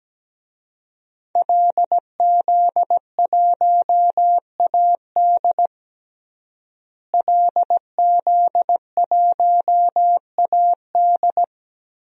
Morse code, heard as a keyed tone, about 17 wpm, 700 Hz. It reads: LZ1AD LZ1AD
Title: DXCC-100 Call Signs - 5 Characters - Repeated 2x - 17wpm